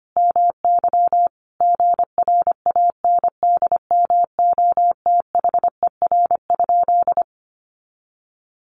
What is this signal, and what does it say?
Morse code, heard as a keyed tone, about 25 wpm, 700 Hz. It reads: MY GRANDMOTHER?
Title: Sentences from Top 1200 Words 25wpm